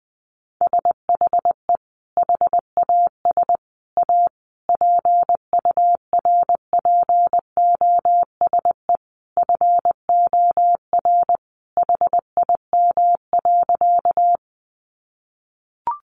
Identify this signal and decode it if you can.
Morse code, heard as a keyed tone, about 20 wpm, 700 Hz. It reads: SHE HAS A PURPOSE FOR HIM.
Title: Sentences from Top 800 Words 20wpm